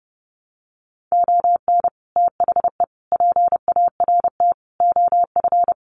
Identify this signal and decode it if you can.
Morse code, heard as a keyed tone, about 30 wpm, 700 Hz. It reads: ON THE PART OF
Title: Sets of 4 Words 30wpm